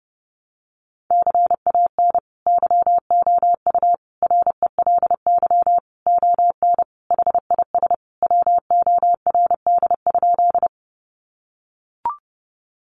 Morse code, heard as a keyed tone, about 30 wpm, 700 Hz. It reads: CAN YOU RELY ON HIS WORD?
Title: Sentences from Top 1200 Words 30wpm